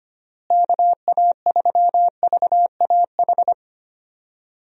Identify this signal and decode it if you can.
Morse code, heard as a keyed tone, about 25 wpm, 700 Hz. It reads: KA3VAH
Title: USA Call Signs - 6 Characters - Repeated 2x - 25wpm